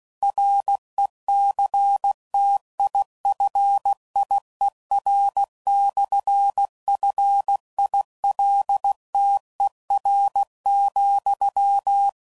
Morse code, heard as a keyed tone, about 16 wpm, 800 Hz. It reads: RECTIFIER/FILTER,
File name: CW Code Groups @15WPM/Technical Subjects